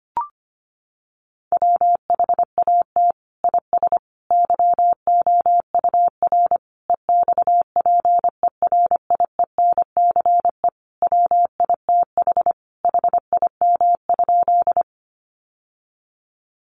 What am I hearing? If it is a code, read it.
Morse code, heard as a keyed tone, about 25 wpm, 700 Hz. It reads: WHAT IS YOUR EXPERIENCE WITH HIM?
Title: Sentences from Top 600 Words 25wpm